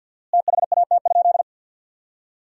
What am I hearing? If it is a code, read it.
Morse code, heard as a keyed tone, about 50 wpm, 700 Hz. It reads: THAT?
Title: 4x Word-Spacing - Sentences from Top 300 Words - 50wpm